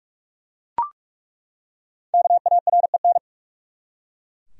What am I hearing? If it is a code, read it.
Morse code, heard as a keyed tone, about 42 wpm, 700 Hz. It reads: KAREN
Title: USA Common Names - Max 5 Letters - 45wpm